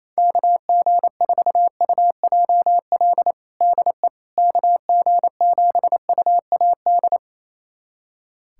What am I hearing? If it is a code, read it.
Morse code, heard as a keyed tone, about 28 wpm, 700 Hz. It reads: KG4UJL DE KG7UAD
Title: USA Callsign DE Callsign 28wpm